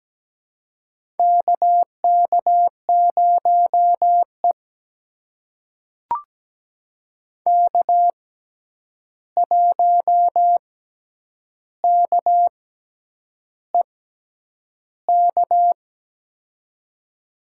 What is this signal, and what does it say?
Morse code, heard as a keyed tone, about 17 wpm, spaced as for 8 wpm, 700 Hz. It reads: KK0E K1KEK
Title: USA Call Signs - 6x Character Spacing - 17wpm